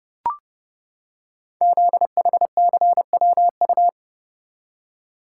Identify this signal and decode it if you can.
Morse code, heard as a keyed tone, about 30 wpm, 700 Hz. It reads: ZHCWU
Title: Code Groups - 5 Letters - 30wpm